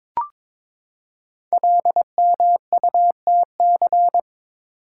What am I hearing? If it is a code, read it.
Morse code, heard as a keyed tone, about 22 wpm, 700 Hz. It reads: LMUTC
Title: Code Groups - 5 Letters - 22wpm